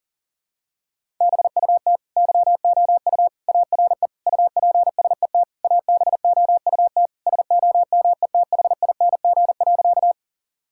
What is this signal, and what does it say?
Morse code, heard as a keyed tone, about 40 wpm, 700 Hz. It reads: BUT YOU ARE UPSET ABOUT SOMETHING.
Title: Sentences from Top 1000 Words - Review Part 5 - 40wpm